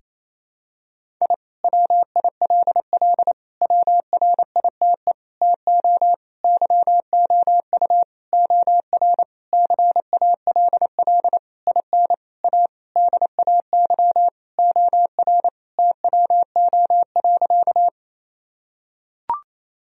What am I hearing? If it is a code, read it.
Morse code, heard as a keyed tone, about 28 wpm, 700 Hz. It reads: I WILL WRITE TO YOU OR CALL IN A DAY OR TWO.
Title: Sentences from Top 1000 Words - Review Part 1 - 28wpm